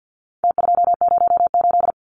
Morse code, heard as a keyed tone, about 49 wpm, 700 Hz. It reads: T208
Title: Numbers - Contest Style - 4-Digit 50wpm